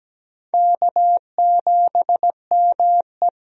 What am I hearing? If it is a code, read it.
Morse code, heard as a keyed tone, about 17 wpm, 700 Hz. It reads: K7ME
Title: USA Call Signs - 4 Characters - 17wpm